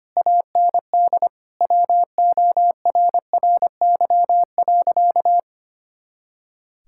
Morse code, heard as a keyed tone, about 25 wpm, 700 Hz. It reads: AND WORRY.
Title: Pangram Challenge v2 - 25wpm